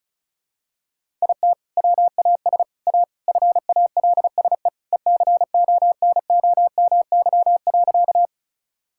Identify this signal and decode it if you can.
Morse code, heard as a keyed tone, about 34 wpm, 700 Hz. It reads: IT WAS A FALSE ECONOMY.
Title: Sentences from Top 1600 Words 35wpm